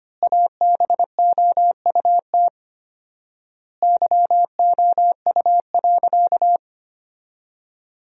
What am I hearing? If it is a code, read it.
Morse code, heard as a keyed tone, about 25 wpm, 700 Hz. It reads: ABOUT YOU.
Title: 4x Word-Spacing - Sentences from Top 400 Words - 25wpm